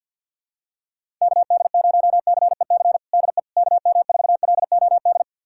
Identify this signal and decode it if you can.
Morse code, heard as a keyed tone, about 52 wpm, 700 Hz. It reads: KD0CEC DE KM4LOD
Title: USA Callsign DE Callsign - 6 Characters - 50wpm